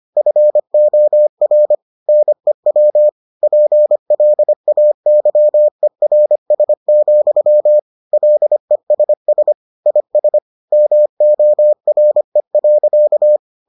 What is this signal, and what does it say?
Morse code, heard as a keyed tone, about 25 wpm, 600 Hz. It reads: FOR NEW PLAYERS, LESS IS MORE.